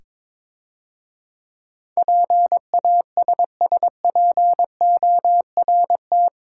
Morse code, heard as a keyed tone, about 22 wpm, 700 Hz. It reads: PASSPORT
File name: Easy Words - Speed-Racing with Hints - 22wpm